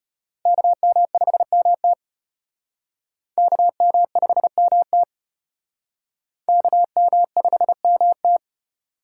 Morse code, heard as a keyed tone, about 34 wpm, 700 Hz. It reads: KM5MT KM5MT KM5MT